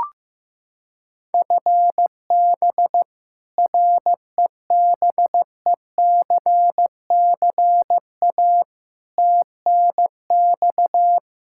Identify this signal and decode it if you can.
Morse code, heard as a keyed tone, about 15 wpm, 700 Hz. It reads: FB REBECCA TNX
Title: Practice QSO - Exchange 3 - More Info - 15wpm